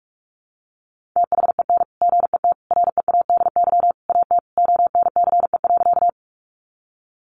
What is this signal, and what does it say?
Morse code, heard as a keyed tone, about 44 wpm, 700 Hz. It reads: THEN GET READY AT ONCE.
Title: Sentences from Top 400 Words 45wpm